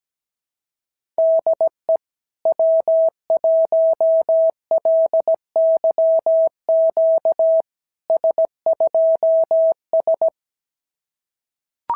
Morse code, heard as a keyed tone, about 17 wpm, 650 Hz. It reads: DE W1LYQ S2S
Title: SOTA QSO Practice 17wpm